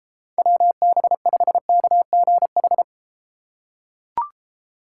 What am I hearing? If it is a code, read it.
Morse code, heard as a keyed tone, about 33 wpm, 700 Hz. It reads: WB5KGH